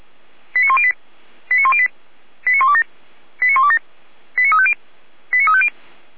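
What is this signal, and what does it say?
Five-tone selective call (ZVEI), heard as quick runs of five tones, tones of about 70 ms. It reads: 89108 89108 89127 89127 89350 89350
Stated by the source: Five-tone ZVEI selective call received on an SDR.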